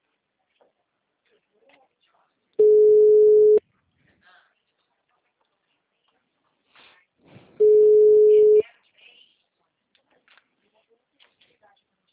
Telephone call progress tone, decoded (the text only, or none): ringback tone